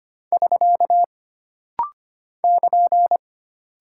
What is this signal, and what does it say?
Morse code, heard as a keyed tone, about 25 wpm, 700 Hz. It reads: <SK> <KN>
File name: Single Punctuation-Prosign - Rapid-Fire - 25wpm